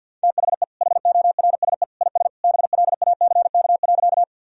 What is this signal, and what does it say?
Morse code, heard as a keyed tone, about 50 wpm, 700 Hz. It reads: THE HORSE IS BLACK.
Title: Sentences from Top 1000 Words - Review Part 2 - 50wpm